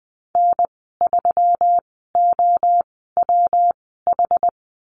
Morse code, heard as a keyed tone, about 20 wpm, spaced as for 15 wpm, 700 Hz. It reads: N3OWH
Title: 179 Lesson 29g - Slash - Call Signs 20wpm